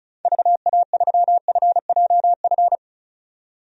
Morse code, heard as a keyed tone, about 36 wpm, 700 Hz. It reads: VA3FJF
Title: DXCC-100 Call Signs - 6 Characters - 35wpm